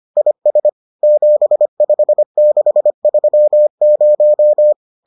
Morse code, heard as a keyed tone, about 25 wpm, 600 Hz. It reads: IS 75630